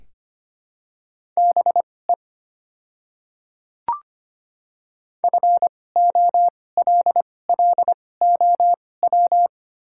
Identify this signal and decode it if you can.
Morse code, heard as a keyed tone, about 25 wpm, spaced as for 18 wpm, 700 Hz. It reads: BE FOLLOW
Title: 2x Character Spacing - Top 200 Words - 25wpm